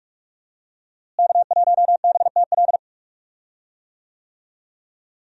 Morse code, heard as a keyed tone, about 44 wpm, 700 Hz. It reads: K1BTL
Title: USA Call Signs - 5 Characters - Repeated 2x - 45wpm